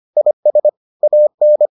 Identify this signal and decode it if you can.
Morse code, heard as a keyed tone, about 25 wpm, 600 Hz. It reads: IS AN